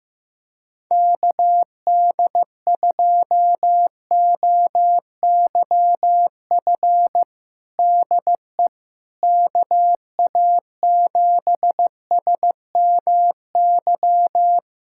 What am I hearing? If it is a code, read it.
Morse code, heard as a keyed tone, about 15 wpm, 700 Hz. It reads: KD2OYF DE KA7SMY
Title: USA Callsign DE Callsign - 6 Characters - 15wpm